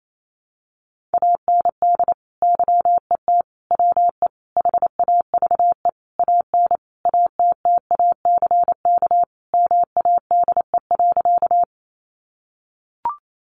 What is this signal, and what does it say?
Morse code, heard as a keyed tone, about 28 wpm, 700 Hz. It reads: AND YET WE HAVE AN ATTACK MADE.
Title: Sentences from Top 1500 Words 28wpm